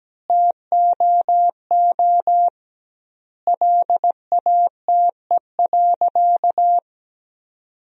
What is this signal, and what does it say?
Morse code, heard as a keyed tone, about 17 wpm, 700 Hz. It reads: TOO LATE.
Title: 2x Word-Spacing - Sentences from Top 300 Words - 17wpm